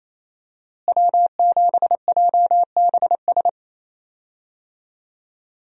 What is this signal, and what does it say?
Morse code, heard as a keyed tone, about 27 wpm, 700 Hz. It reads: W7JBS
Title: USA Call Signs - 5 Characters - Repeated 2x - 28wpm